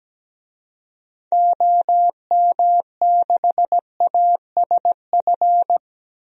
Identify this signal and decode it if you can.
Morse code, heard as a keyed tone, about 17 wpm, 700 Hz. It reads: OM6ASF
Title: DXCC-100 Call Signs - 6 Characters - Repeated 3x - 17wpm